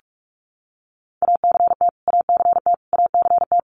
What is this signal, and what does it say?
Morse code, heard as a keyed tone, about 44 wpm, 700 Hz. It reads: ACT ACT ACT
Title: Top 200 Words Repeated 45wpm